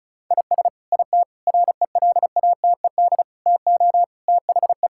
Morse code, heard as a keyed tone, about 34 wpm, 700 Hz. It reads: IS IT RELATED TO THE
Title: Sentences from Top 1000 Words - Review Part 5 - 35wpm